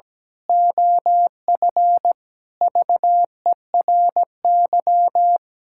Morse code, heard as a keyed tone, about 17 wpm, 700 Hz. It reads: OF VERY
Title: Pangram Challenge v2 - 17wpm